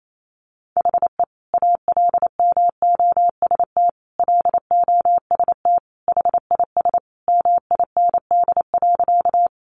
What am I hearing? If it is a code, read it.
Morse code, heard as a keyed tone, about 28 wpm, 700 Hz. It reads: HE ALMOST LOST HIS MIND.
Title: Sentences from Top 1400 Words 28wpm